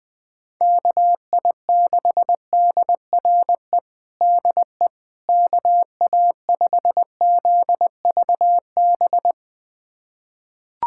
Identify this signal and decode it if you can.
Morse code, heard as a keyed tone, about 20 wpm, 700 Hz. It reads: KI6DRE DE KA5ZVB
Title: USA Callsign DE Callsign 20wpm